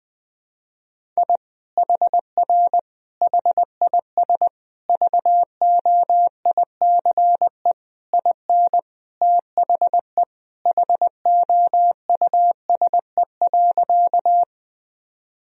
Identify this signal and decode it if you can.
Morse code, heard as a keyed tone, about 21 wpm, 700 Hz. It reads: I HR HIS VOICE IN THE HOUSE.
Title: Sentences from Top 500 Words and Abbreviations 20wpm